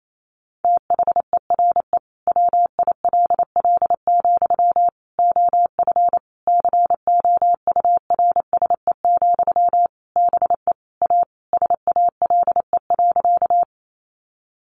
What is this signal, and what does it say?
Morse code, heard as a keyed tone, about 28 wpm, 700 Hz. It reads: THERE WILL, OF COURSE, BE A SALE.
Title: Sentences from Top 1400 Words 28wpm